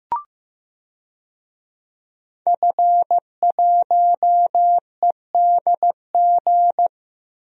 Morse code, heard as a keyed tone, about 15 wpm, 700 Hz. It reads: F1EDG
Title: DXCC-100 Call Signs - 5 Characters - Repeated 2x - 15wpm